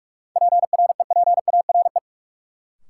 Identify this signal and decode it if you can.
Morse code, heard as a keyed tone, about 45 wpm, 700 Hz. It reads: PREPARE
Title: Sentences from Top 900 Words - New Word Drill - 45wpm